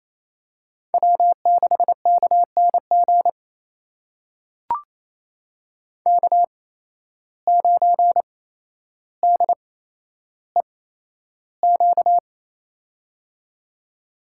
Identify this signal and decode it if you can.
Morse code, heard as a keyed tone, about 27 wpm, spaced as for 12 wpm, 700 Hz. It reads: W6KNG K9DEQ
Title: USA Call Signs - 5 Characters - 8x Character Spacing - 28wpm